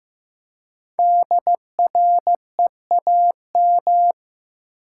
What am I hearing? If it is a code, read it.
Morse code, heard as a keyed tone, about 15 wpm, 700 Hz. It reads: DREAM
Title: Five Letter Words - 15wpm